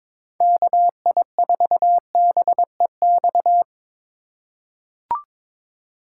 Morse code, heard as a keyed tone, about 22 wpm, 700 Hz. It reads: KI4BEX
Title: USA Call Signs - 6 Characters - 3x Character Spacing - 22wpm